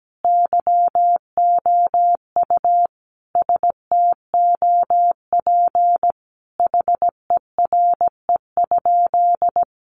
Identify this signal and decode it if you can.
Morse code, heard as a keyed tone, about 17 wpm, 700 Hz. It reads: YOU STOP HERE?
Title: Sentences from Top 300 Words 17wpm